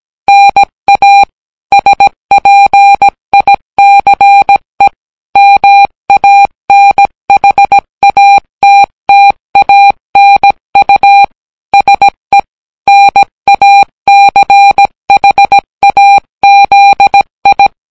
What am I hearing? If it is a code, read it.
Morse code, heard as a keyed tone, about 17 wpm, 800 Hz. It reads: NA SPICE MANHATTANU SE NACHAZI